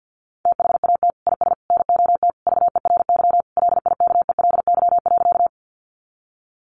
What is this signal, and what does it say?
Morse code, heard as a keyed tone, about 50 wpm, 700 Hz. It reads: THAT IS NOT VERY LIKELY.
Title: Sentences from Top 600 Words 50wpm